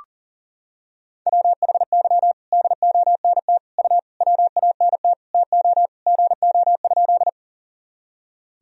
Morse code, heard as a keyed tone, about 40 wpm, 700 Hz. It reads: WHY DONT U WANT TO GO?
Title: Sentences from Top 300 Words and Abbreviations 40wpm